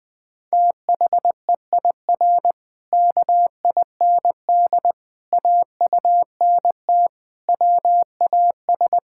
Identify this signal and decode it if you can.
Morse code, heard as a keyed tone, about 20 wpm, 700 Hz. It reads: THEIR KIND AUNT WAS
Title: Pangram Challenge v2 - 20wpm